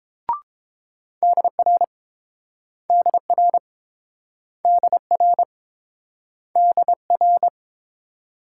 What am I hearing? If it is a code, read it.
Morse code, heard as a keyed tone, about 30 wpm, 700 Hz. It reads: DR DR DR DR